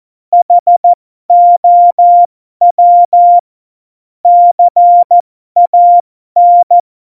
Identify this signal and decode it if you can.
Morse code, heard as a keyed tone, about 14 wpm, spaced as for 12 wpm, 700 Hz. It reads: HOW CAN
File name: Morsekurs / Witze